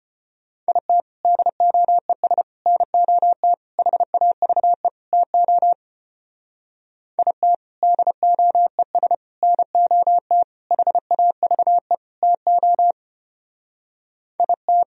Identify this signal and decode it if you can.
Morse code, heard as a keyed tone, about 32 wpm, 700 Hz. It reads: IT DOES NOT HAVE TO IT DOES NOT HAVE TO IT